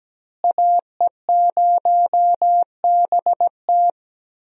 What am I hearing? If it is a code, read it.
Morse code, heard as a keyed tone, about 17 wpm, 700 Hz. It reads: AE0BT
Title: USA Call Signs - 5 Characters - Repeated 2x - 17wpm